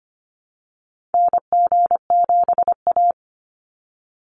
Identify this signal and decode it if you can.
Morse code, heard as a keyed tone, about 25 wpm, 700 Hz. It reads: NG7A
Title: Call Signs Repeated 3x - 25wpm